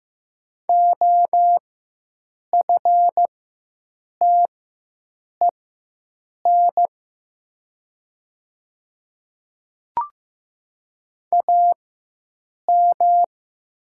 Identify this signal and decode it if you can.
Morse code, heard as a keyed tone, about 15 wpm, spaced as for 7 wpm, 700 Hz. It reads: OFTEN AM
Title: 4x Character Spacing - Top 500 Words - 15wpm